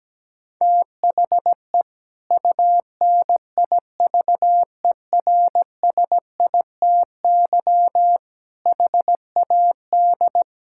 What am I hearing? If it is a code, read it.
Morse code, heard as a keyed tone, about 17 wpm, 700 Hz. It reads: THE UNIVERSITY HAD
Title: Sentences from Top 1000 Words - Review Part 5 - 17wpm